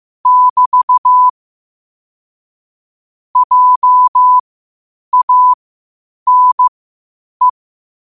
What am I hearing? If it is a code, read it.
Morse code, heard as a keyed tone, about 15 wpm, spaced as for 8 wpm, 1000 Hz. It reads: <BT> JANE
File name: qotdfiles / 8wpm